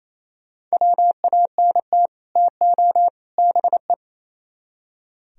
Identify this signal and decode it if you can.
Morse code, heard as a keyed tone, about 37 wpm, 700 Hz. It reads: WANT TO BE